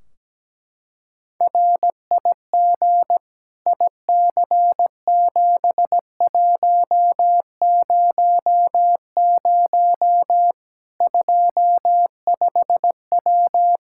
Morse code, heard as a keyed tone, about 17 wpm, 700 Hz. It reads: RIG IC7100 25W
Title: K1USN SST QSO Practice 17wpm